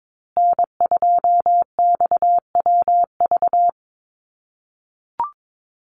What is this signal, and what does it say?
Morse code, heard as a keyed tone, about 22 wpm, 700 Hz. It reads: N2XWV
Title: USA Call Signs Repeated 3x 22wpm